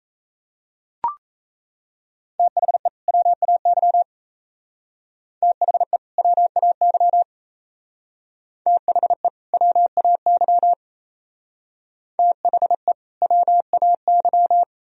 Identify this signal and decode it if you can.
Morse code, heard as a keyed tone, about 35 wpm, 700 Hz. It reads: THE WAY THE WAY THE WAY THE WAY
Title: Speed-Racing - Sets of 2 Words 28wpm